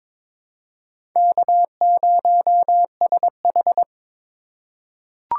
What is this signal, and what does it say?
Morse code, heard as a keyed tone, about 22 wpm, 700 Hz. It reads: K0SH
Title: USA Call Signs - 4 Characters - 4x Character Spacing - 22wpm